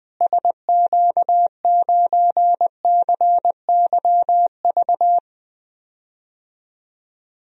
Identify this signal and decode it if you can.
Morse code, heard as a keyed tone, about 21 wpm, 700 Hz. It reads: SQ9CYV
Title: DXCC-100 Call Signs - 6 Characters - Repeated 3x - 20wpm